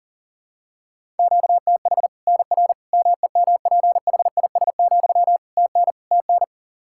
Morse code, heard as a keyed tone, about 40 wpm, 700 Hz. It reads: QTH NR MEMPHIS, TN TN
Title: Practice QSO - Exchange 1 - Beginning - 40wpm